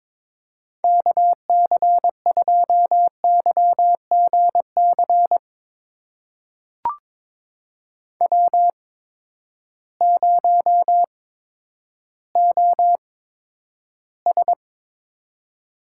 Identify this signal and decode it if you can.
Morse code, heard as a keyed tone, about 22 wpm, spaced as for 11 wpm, 700 Hz. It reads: KC2YGC W0OS
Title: USA Call Signs - 8x Character Spacing - 22wpm